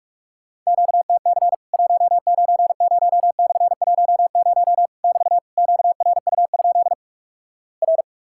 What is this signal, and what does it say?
Morse code, heard as a keyed tone, about 46 wpm, 700 Hz. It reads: QTC 190/10 <BT> QRU? R
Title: Worked All Europe DX Contest - QTC Exchange - 45wpm